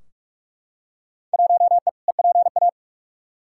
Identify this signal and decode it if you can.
Morse code, heard as a keyed tone, about 45 wpm, 700 Hz. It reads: 1E EPA
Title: Field Day Exchanges - 45wpm